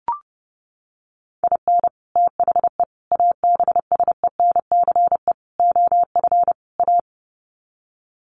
Morse code, heard as a keyed tone, about 30 wpm, 700 Hz. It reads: IN THE ABSENCE OF A
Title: Sets of 5 Words Encore 30wpm